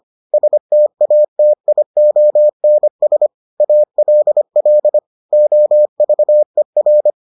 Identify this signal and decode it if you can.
Morse code, heard as a keyed tone, about 25 wpm, 600 Hz. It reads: STATIONS ALL OVER